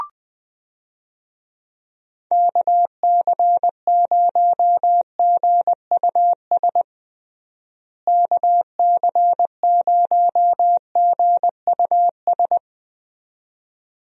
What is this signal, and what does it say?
Morse code, heard as a keyed tone, about 20 wpm, 700 Hz. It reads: KC0GUS KC0GUS
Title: USA Call Signs - 6 Characters - Repeated 2x - 20wpm